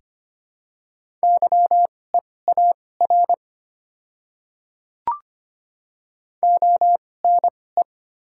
Morse code, heard as a keyed tone, about 25 wpm, spaced as for 18 wpm, 700 Hz. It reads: YEAR ONE
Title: 2x Character Spacing - Top 200 Words - 25wpm